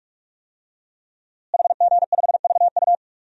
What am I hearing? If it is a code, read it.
Morse code, heard as a keyed tone, about 41 wpm, 700 Hz. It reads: HG5VU
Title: DXCC-100 Call Signs - 5 Characters - Repeated 3x - 45wpm